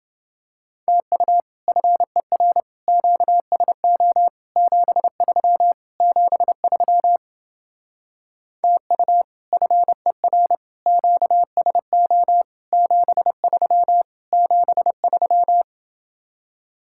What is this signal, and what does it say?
Morse code, heard as a keyed tone, about 29 wpm, 700 Hz. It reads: TU FER QSO 73 73 TU FER QSO 73 73